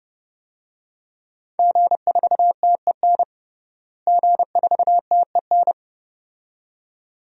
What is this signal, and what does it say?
Morse code, heard as a keyed tone, about 30 wpm, 700 Hz. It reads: G4TEN G4TEN
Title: DXCC-100 Call Signs - 5 Characters - Repeated 2x - 30wpm